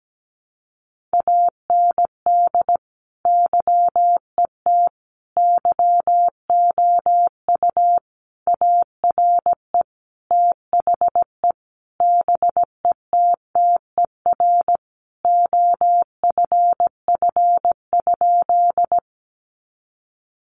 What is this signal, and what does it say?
Morse code, heard as a keyed tone, about 17 wpm, 700 Hz. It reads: AND YET YOU ARE THE BETTER OFF?
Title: Sentences from Top 500 Words 17wpm